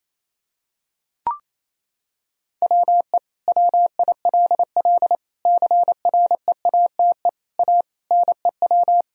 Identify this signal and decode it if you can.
Morse code, heard as a keyed tone, about 28 wpm, 700 Hz. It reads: WE WILL CREATE A NEW
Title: Sentences from Top 1000 Words - Review Part 3 - 28wpm